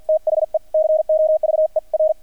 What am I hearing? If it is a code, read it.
Morse code, heard as a keyed tone, about 50 wpm, spaced as for 44 wpm, 630 Hz, with radio noise. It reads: THE COVER
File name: archived / CRSnet